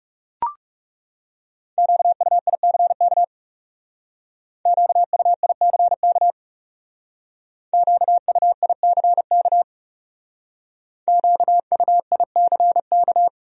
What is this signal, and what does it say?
Morse code, heard as a keyed tone, about 38 wpm, 700 Hz. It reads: QUICK QUICK QUICK QUICK